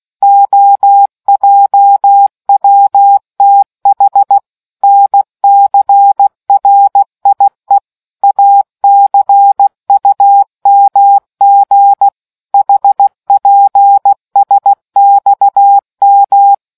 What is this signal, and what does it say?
Morse code, heard as a keyed tone, about 16 wpm, 800 Hz. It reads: OJWTH NCRIE ACUMG HPSXM